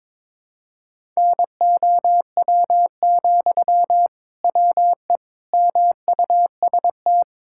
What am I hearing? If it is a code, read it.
Morse code, heard as a keyed tone, about 22 wpm, 700 Hz. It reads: NOW, WE MUST
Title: Sentences from Top 1000 Words - Review Part 6 - 22wpm